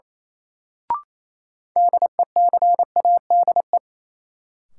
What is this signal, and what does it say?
Morse code, heard as a keyed tone, about 27 wpm, 700 Hz. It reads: DECADE
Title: Sets of 5 Words - New Word Drill - 28wpm